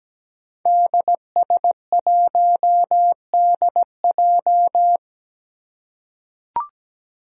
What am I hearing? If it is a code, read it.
Morse code, heard as a keyed tone, about 17 wpm, 700 Hz. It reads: DS1DJ
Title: Five Challenging Letter-Number Combinations 17wpm